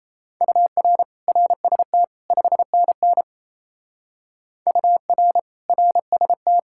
Morse code, heard as a keyed tone, about 34 wpm, 700 Hz. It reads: UR RST 5NN UR RST
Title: Speed-Racing Common QSO Phrases Encore - 28wpm